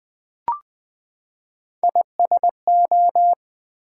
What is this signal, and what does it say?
Morse code, heard as a keyed tone, about 20 wpm, 700 Hz. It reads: ISO
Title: Build A Word - 3 Letters At A Time - 9-Letter Words - 20wpm